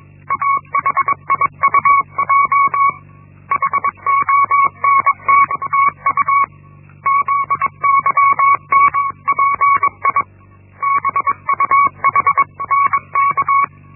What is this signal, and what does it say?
Morse code, heard as a keyed tone, about 21 wpm, 1120 Hz. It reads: AHIUJ HONTTU ZYNPI BUHNK
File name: Linksmakalnis-1